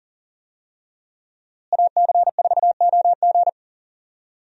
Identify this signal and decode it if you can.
Morse code, heard as a keyed tone, about 40 wpm, 700 Hz. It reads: AC4OG
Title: USA Call Signs - 5 Characters - Repeated 3x - 40wpm